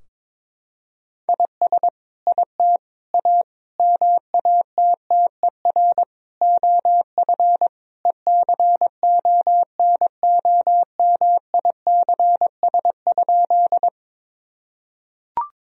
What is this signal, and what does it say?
Morse code, heard as a keyed tone, about 22 wpm, 700 Hz. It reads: IS IT A MATTER OF ECONOMICS?
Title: Sentences from Top 1000 Words - Review Part 3 - 22wpm